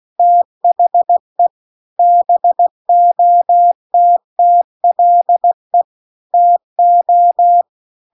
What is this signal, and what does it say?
Morse code, heard as a keyed tone, about 16 wpm, 700 Hz. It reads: THE BOTTLE TO